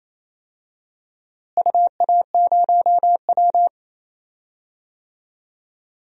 Morse code, heard as a keyed tone, about 28 wpm, 700 Hz. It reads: UA0W
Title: DXCC-100 Call Signs - 4 Characters - Repeated 2x - 28wpm